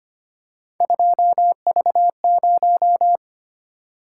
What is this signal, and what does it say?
Morse code, heard as a keyed tone, about 25 wpm, 700 Hz. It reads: 2V0
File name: Three Letter-Number Combinations - 25wpm